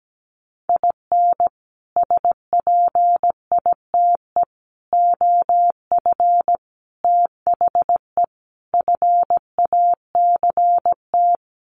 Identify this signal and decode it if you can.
Morse code, heard as a keyed tone, about 17 wpm, 700 Hz. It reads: IN SPITE OF THE FACT
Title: Sets of 5 Words Encore2 17wpm